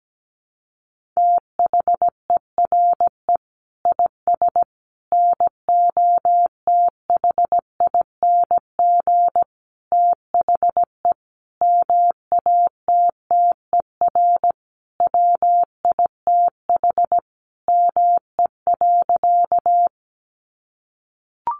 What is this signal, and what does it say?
Morse code, heard as a keyed tone, about 17 wpm, 700 Hz. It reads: THERE IS NOTHING THE MATTER WITH ME.
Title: Sentences from Top 700 Words 17wpm